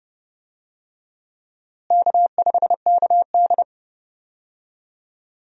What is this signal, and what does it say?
Morse code, heard as a keyed tone, about 29 wpm, 700 Hz. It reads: K5KD
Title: USA Call Signs - 4 Characters - Repeated 2x - 30wpm